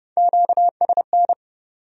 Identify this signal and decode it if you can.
Morse code, heard as a keyed tone, about 39 wpm, 700 Hz. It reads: QSN